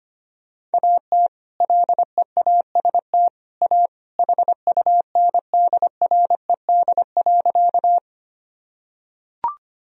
Morse code, heard as a keyed tone, about 25 wpm, 700 Hz. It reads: AT LEAST A HUNDRED.
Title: Sentences from Top 1000 Words - Review Part 2 - 25wpm